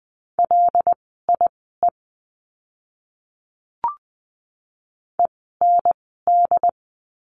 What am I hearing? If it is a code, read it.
Morse code, heard as a keyed tone, about 20 wpm, spaced as for 15 wpm, 700 Hz. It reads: LIE END
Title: 094 Lesson 16f - Letter c - Words 20wpm